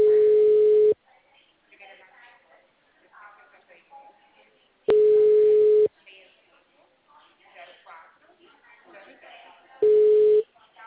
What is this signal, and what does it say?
Ringback tone (Europe, 425 Hz), with speech.